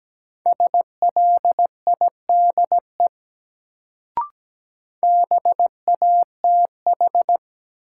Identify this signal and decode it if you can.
Morse code, heard as a keyed tone, about 17 wpm, 700 Hz. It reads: SLIDE BATH
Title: Sentences from Top 1600 Words - New Word Drill - 17wpm